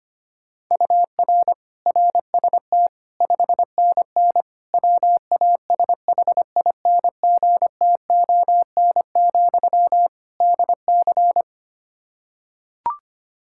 Morse code, heard as a keyed tone, about 25 wpm, 700 Hz. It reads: UR RST 5NN WASHINGTON, DC
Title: Practice QSO - Exchange 1 - Beginning - 25wpm